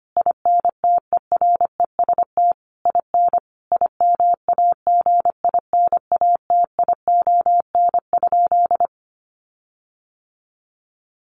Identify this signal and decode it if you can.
Morse code, heard as a keyed tone, about 25 wpm, 700 Hz. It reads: INTEREST IN IMAGINATION?
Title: Sentences from Top 1300 Words 25wpm